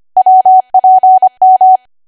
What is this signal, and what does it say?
Morse code, heard as a keyed tone, about 25 wpm, 750 Hz. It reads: WPM
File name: Archive / 25 WPM / 25WPM